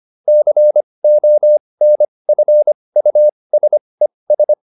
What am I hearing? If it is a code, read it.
Morse code, heard as a keyed tone, about 25 wpm, spaced as for 20 wpm, 600 Hz. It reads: CONFUSES
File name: ES8 20